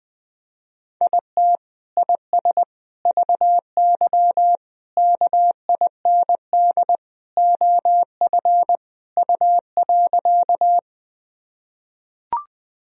Morse code, heard as a keyed tone, about 21 wpm, 700 Hz. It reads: IT IS VY KIND OF U.
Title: Sentences from Top 200 Words and Abbreviations 20wpm